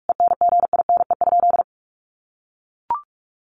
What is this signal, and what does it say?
Morse code, heard as a keyed tone, about 44 wpm, 700 Hz. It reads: ENGINE?
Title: Sentences from Top 1500 Words 45wpm